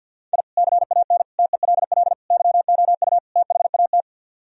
Morse code, heard as a keyed tone, about 62 wpm, 700 Hz. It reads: I CAN TELL YOU THAT
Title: Speed-Racing - Sets of 5 Words 50wpm